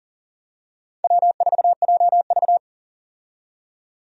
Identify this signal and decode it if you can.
Morse code, heard as a keyed tone, about 40 wpm, 700 Hz. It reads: W4JV
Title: USA Call Signs - 4 Characters - 40wpm